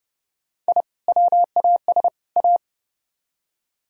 Morse code, heard as a keyed tone, about 40 wpm, 700 Hz. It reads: I WAS A